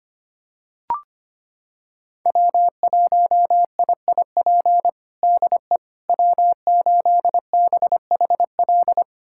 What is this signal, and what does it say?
Morse code, heard as a keyed tone, about 25 wpm, 700 Hz. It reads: W1IIP DE W8BHL
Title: USA Callsign DE Callsign - 5 Characters - 25wpm